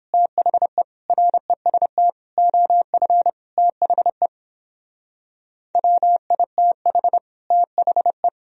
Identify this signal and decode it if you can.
Morse code, heard as a keyed tone, about 29 wpm, 700 Hz. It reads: THE REST OF THE WITH THE